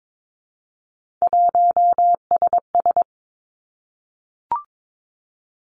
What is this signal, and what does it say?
Morse code, heard as a keyed tone, about 22 wpm, 700 Hz. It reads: 1SS
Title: Three Challenging Letter-Number Combinations 22wpm